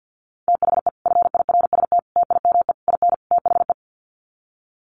Morse code, heard as a keyed tone, about 50 wpm, 700 Hz. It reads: THE FIRST TIME IN THE
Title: Sets of 5 Words Encore 50wpm